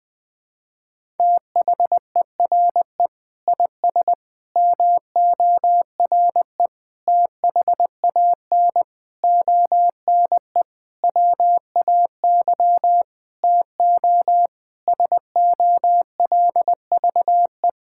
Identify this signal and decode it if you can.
Morse code, heard as a keyed tone, about 20 wpm, 700 Hz. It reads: THERE IS MORE THAN ONE WAY TO SOLVE
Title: Sentences with 9-Word Phrases - 20wpm